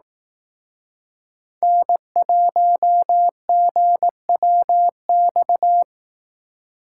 Morse code, heard as a keyed tone, about 19 wpm, 700 Hz. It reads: N1GWX
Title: Speed Racing - USA 1x3 Call Signs 15wpm